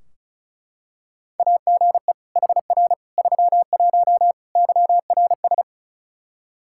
Morse code, heard as a keyed tone, about 36 wpm, 700 Hz. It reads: AGE HR 31 YRS
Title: Practice QSO - Exchange 3 - More Info - 35wpm